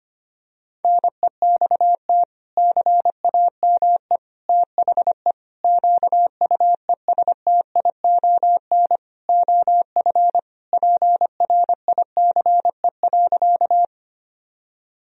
Morse code, heard as a keyed tone, about 25 wpm, 700 Hz. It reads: NEXT CAME THE QUESTION OF PRICE.
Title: Sentences from Top 1000 Words - Review Part 4 - 25wpm